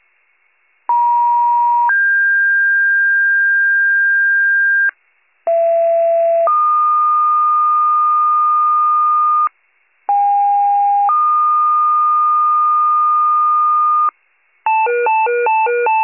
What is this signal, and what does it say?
Two-tone page, heard as a one-second tone, then a three-second tone, about 950 then 1600 Hz, about 670 then 1220 Hz, about 810 then 1220 Hz. It is heard from 0.9 to 14.1 s.